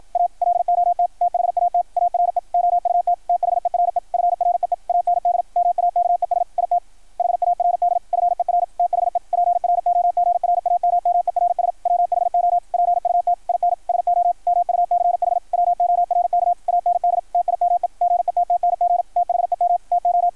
Morse code, heard as a keyed tone, about 53 wpm, 660 Hz, with radio noise. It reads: A LOT THAT ARE OUT THERE FREE AND MAKES IT HARD FER THE PROGRAMMERS WHO PUT IN SO MUCH WORK AND TIME GETTNG THEM TO
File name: archived / CRSnet